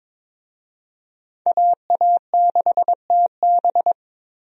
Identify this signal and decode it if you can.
Morse code, heard as a keyed tone, about 22 wpm, 700 Hz. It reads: AA6TB
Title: USA Call Signs - 5 Characters - Repeated 2x - 22wpm